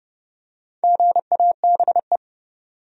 Morse code, heard as a keyed tone, about 30 wpm, 700 Hz. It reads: GABE
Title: USA Common Names - Max 4 Letters - 30wpm